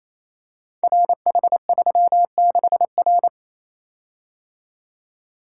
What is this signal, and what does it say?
Morse code, heard as a keyed tone, about 27 wpm, 700 Hz. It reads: RH36R
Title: Code Groups - 5 Letter:Numbers - 28wpm